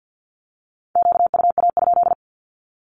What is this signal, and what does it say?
Morse code, heard as a keyed tone, about 53 wpm, 700 Hz. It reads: QUA?
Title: Q-Codes 50wpm